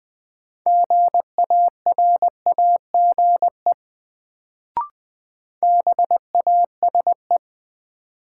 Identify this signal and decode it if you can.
Morse code, heard as a keyed tone, about 20 wpm, 700 Hz. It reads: GARAGE BASE
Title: Sentences from Top 1600 Words - New Word Drill Alt - 20wpm